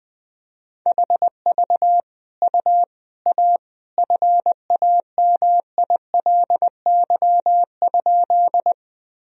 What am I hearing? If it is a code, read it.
Morse code, heard as a keyed tone, about 21 wpm, 700 Hz. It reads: HV U A FAMILY?
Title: Sentences from Top 400 Words and Abbreviations 20wpm